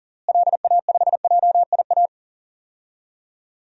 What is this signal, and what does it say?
Morse code, heard as a keyed tone, about 40 wpm, 700 Hz. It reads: LA5JIA
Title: DXCC-100 Call Signs - 6 Characters - Repeated 2x - 40wpm